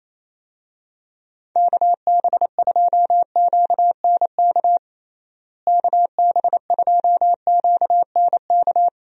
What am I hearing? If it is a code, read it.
Morse code, heard as a keyed tone, about 28 wpm, 700 Hz. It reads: KB2QNK KB2QNK
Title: USA Call Signs - 6 Characters - Repeated 3x - 28wpm